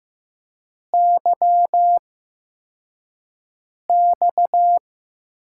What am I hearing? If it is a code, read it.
Morse code, heard as a keyed tone, about 15 wpm, spaced as for 4 wpm, 700 Hz. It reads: YX
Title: USA Call Signs - 6 Characters - 8x Character Spacing - 15wpm